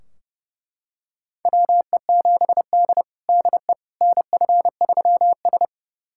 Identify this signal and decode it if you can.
Morse code, heard as a keyed tone, about 32 wpm, 700 Hz. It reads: WE7D DE NF3S
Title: USA Callsign DE Callsign - 4 Characters - 30wpm